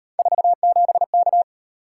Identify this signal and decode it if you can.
Morse code, heard as a keyed tone, about 38 wpm, 700 Hz. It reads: 47K